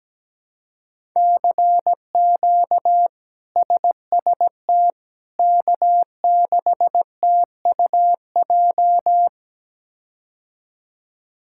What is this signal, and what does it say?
Morse code, heard as a keyed tone, about 17 wpm, 700 Hz. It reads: CQ SST K6TUJ
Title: K1USN SST QSO Practice - 17wpm